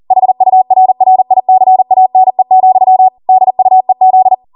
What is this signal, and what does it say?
Morse code, heard as a keyed tone, about 40 wpm, 750 Hz. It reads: HURRICANE, DUEZ